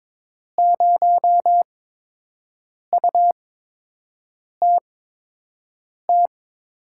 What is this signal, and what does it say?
Morse code, heard as a keyed tone, about 22 wpm, spaced as for 6 wpm, 700 Hz. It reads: 0UTT
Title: USA Call Signs - 6 Characters - 8x Character Spacing - 22wpm